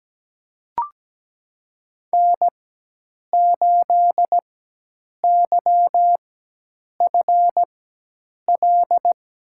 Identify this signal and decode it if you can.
Morse code, heard as a keyed tone, about 17 wpm, spaced as for 8 wpm, 700 Hz. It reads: N8YFL
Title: USA Call Signs - 5 Characters - 4x Character Spacing - 17wpm